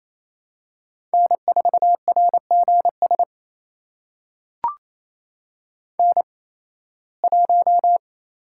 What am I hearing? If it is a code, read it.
Morse code, heard as a keyed tone, about 27 wpm, spaced as for 18 wpm, 700 Hz. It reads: N4RGS N1
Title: USA Call Signs - 5 Characters - 8x Character Spacing - 28wpm